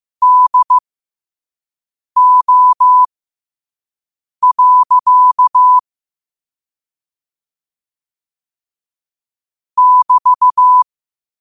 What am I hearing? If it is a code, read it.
Morse code, heard as a keyed tone, about 15 wpm, spaced as for 5 wpm, 1000 Hz. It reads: DO. <BT>
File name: qotdfiles / 5wpm